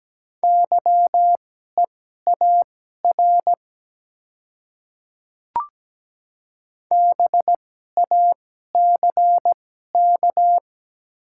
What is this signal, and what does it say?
Morse code, heard as a keyed tone, about 17 wpm, spaced as for 12 wpm, 700 Hz. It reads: YEAR BACK
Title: 2x Character Spacing - Top 300 Words - 17wpm